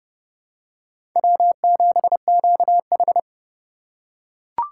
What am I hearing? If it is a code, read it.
Morse code, heard as a keyed tone, about 31 wpm, 700 Hz. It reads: W7QH
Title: USA Call Signs - 4 Characters - 3x Character Spacing - 30wpm